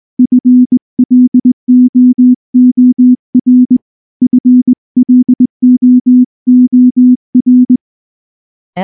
Morse code, heard as a keyed tone, about 20 wpm, 250 Hz. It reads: FLOOR FLOOR
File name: morse / morse-training-20WPM-7